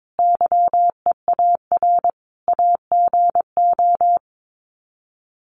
Morse code, heard as a keyed tone, about 22 wpm, 700 Hz. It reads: YEAR AGO
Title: Sets of 5 Words Encore2 22wpm